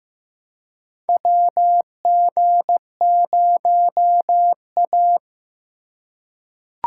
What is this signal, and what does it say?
Morse code, heard as a keyed tone, about 15 wpm, 700 Hz. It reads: WG0A
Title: USA Call Signs - 4 Characters - 3x Character Spacing - 15wpm